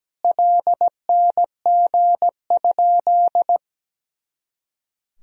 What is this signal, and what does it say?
Morse code, heard as a keyed tone, about 17 wpm, 700 Hz. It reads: LNG?
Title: Sentences from Top 200 Words and Abbreviations 17wpm